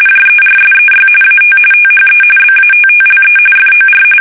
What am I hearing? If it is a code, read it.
RTTY, about 45.45 baud. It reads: JERSEY AT THE BATTLE FOR
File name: fsk850 / battleshipnj